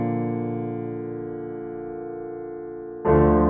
Dial tone (North America), 350 + 440 Hz.